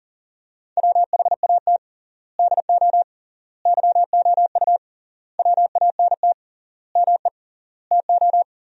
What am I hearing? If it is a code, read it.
Morse code, heard as a keyed tone, about 40 wpm, 700 Hz. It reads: WHAT DO YOU WANT ME TO
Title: 3x Word-Spacing - Sentences from Top 200 Words - 40wpm